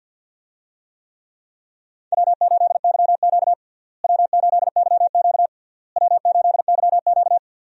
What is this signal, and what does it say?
Morse code, heard as a keyed tone, about 50 wpm, 700 Hz. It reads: W8YQ W8YQ W8YQ
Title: USA Call Signs - 4 Characters - Repeated 3x - 50wpm